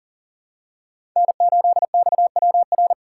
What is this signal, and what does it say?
Morse code, heard as a keyed tone, about 39 wpm, 700 Hz. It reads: N8XWR
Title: USA Call Signs - 2x Character Spacing - 40wpm